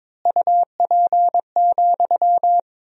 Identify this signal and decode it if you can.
Morse code, heard as a keyed tone, about 22 wpm, 700 Hz. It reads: UP,
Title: 4x Word-Spacing - Sentences from Top 500 Words - 22wpm